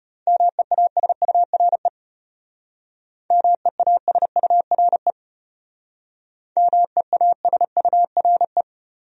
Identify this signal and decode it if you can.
Morse code, heard as a keyed tone, about 33 wpm, 700 Hz. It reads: MEASURE MEASURE MEASURE